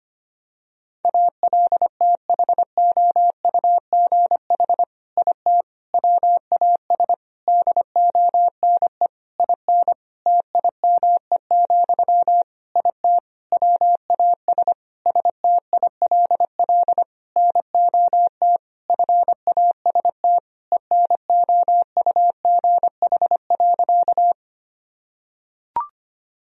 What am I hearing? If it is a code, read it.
Morse code, heard as a keyed tone, about 25 wpm, 700 Hz. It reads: ALTHOUGH IT WAS DONE IN TIME, IT WAS STILL NOT FAST ENOUGH.
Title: Sentences from Top 1000 Words - Review Part 3 - 25wpm